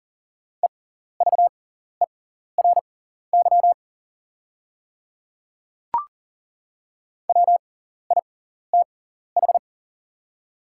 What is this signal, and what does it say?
Morse code, heard as a keyed tone, about 40 wpm, spaced as for 14 wpm, 690 Hz. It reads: EVERY WITH
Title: 6x Character Spacing - Top 200 Words - 40wpm